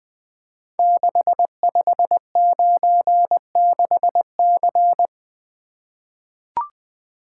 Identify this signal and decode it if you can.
Morse code, heard as a keyed tone, about 20 wpm, 700 Hz. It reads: 6596C
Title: Code Groups - 5 Letter:Numbers - 20wpm